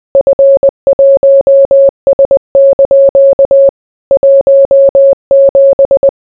Morse code, heard as a keyed tone, about 20 wpm, 550 Hz. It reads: F1S) 17